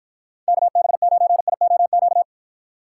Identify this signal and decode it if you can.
Morse code, heard as a keyed tone, about 54 wpm, 700 Hz. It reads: KB9IOQ